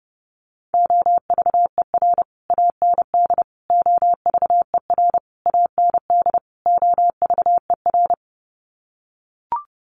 Morse code, heard as a keyed tone, about 30 wpm, 700 Hz. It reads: OVER AND OVER AND OVER
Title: Sets of 5 Words Encore 30wpm